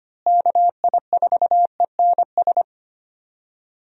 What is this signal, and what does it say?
Morse code, heard as a keyed tone, about 25 wpm, 700 Hz. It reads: KI4ENS
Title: USA Call Signs - 6 Characters - Repeated 3x - 25wpm